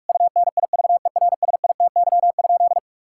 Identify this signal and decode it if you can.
Morse code, heard as a keyed tone, about 41 wpm, 700 Hz. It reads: UNIVERSITY?
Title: Sentences from Top 1000 Words - Review Part 6 - 45wpm